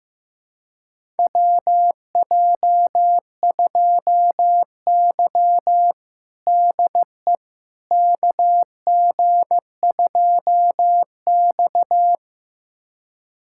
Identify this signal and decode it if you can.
Morse code, heard as a keyed tone, about 15 wpm, 700 Hz. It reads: WJ2Y DE KG2X
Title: USA Callsign DE Callsign - 4 Characters - 15wpm